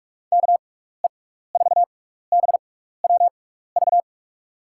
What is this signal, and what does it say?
Morse code, heard as a keyed tone, about 48 wpm, spaced as for 16 wpm, 700 Hz. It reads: KE4BWV
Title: USA Call Signs - 6 Characters - 6x Character Spacing - 45wpm